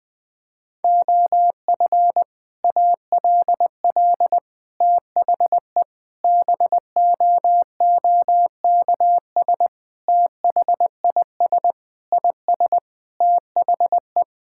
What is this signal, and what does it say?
Morse code, heard as a keyed tone, about 20 wpm, 700 Hz. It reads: OF ALL THE BOOKS THIS IS THE
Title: Sentences with 3-Word Phrases - 20wpm